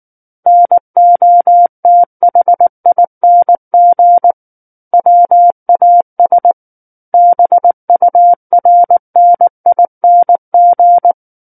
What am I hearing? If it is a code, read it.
Morse code, heard as a keyed tone, about 19 wpm, 700 Hz. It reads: NOTHING WAS BURNING
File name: Book One / 10-Chapter 10 - 19 WPM